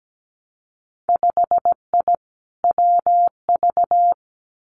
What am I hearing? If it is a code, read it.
Morse code, heard as a keyed tone, about 17 wpm, 700 Hz. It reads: 5I WV
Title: Winter Field Day Exchanges Encore 17wpm